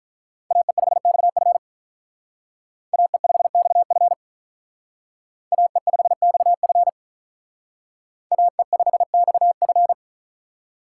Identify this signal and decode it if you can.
Morse code, heard as a keyed tone, about 47 wpm, 700 Hz. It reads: AE5XF AE5XF AE5XF AE5XF